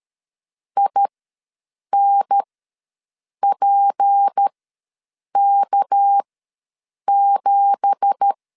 Morse code, heard as a keyed tone, about 13 wpm, spaced as for 7 wpm, 800 Hz. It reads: INPK7
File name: LEZIONI / 10 SIGNS / 10C DOT ws 12wpm 3548